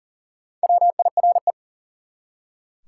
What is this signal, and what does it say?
Morse code, heard as a keyed tone, about 41 wpm, 700 Hz. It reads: WIRE
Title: Top 180 QSO Elements 40wpm